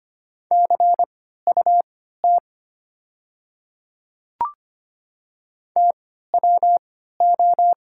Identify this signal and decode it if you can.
Morse code, heard as a keyed tone, about 25 wpm, spaced as for 14 wpm, 700 Hz. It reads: CUT TWO
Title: 3x Character Spacing - Top 300 Words - 25wpm